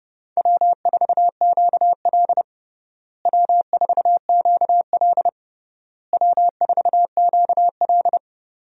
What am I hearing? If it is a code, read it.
Morse code, heard as a keyed tone, about 30 wpm, 700 Hz. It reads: W4QL W4QL W4QL
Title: USA Call Signs - 4 Characters - Repeated 3x - 30wpm